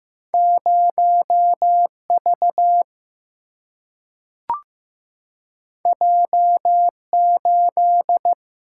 Morse code, heard as a keyed tone, about 15 wpm, 700 Hz. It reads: 0 V J 8
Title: Single-Letter Single-Number 15wpm